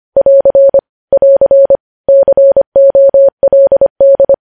Morse code, heard as a keyed tone, about 25 wpm, 550 Hz. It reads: <AR> <AR> COLD
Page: News Headlines in Morse Code at 25 WPM